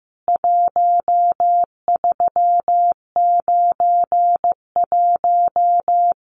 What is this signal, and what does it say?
Morse code, heard as a keyed tone, about 15 wpm, 700 Hz. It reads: 1391
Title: CWT Exchange 15wpm